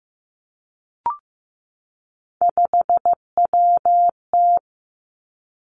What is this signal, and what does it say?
Morse code, heard as a keyed tone, about 15 wpm, 700 Hz. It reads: 5WT
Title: Three Challenging Letter-Number Combinations 15wpm